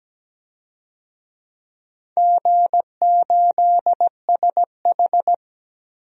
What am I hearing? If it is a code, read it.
Morse code, heard as a keyed tone, about 17 wpm, 700 Hz. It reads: G8SH
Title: DXCC-100 Call Signs - 4 Characters - 17wpm